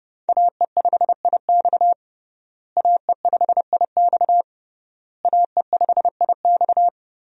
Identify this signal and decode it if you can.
Morse code, heard as a keyed tone, about 30 wpm, 700 Hz. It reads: AE5IX AE5IX AE5IX
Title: USA Call Signs - 5 Characters - Repeated 3x - 30wpm